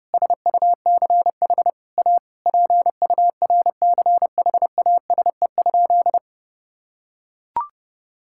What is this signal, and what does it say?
Morse code, heard as a keyed tone, about 32 wpm, 700 Hz. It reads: SUCH A PURCHASE?
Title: Sentences from Top 1000 Words - Review Part 6 - 30wpm